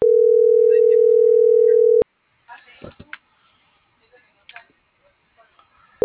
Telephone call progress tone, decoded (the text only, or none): ringback tone